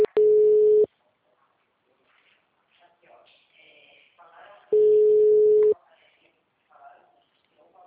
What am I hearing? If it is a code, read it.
Ringback tone (Europe, 425 Hz), with speech.
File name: out- -2014- - - .1850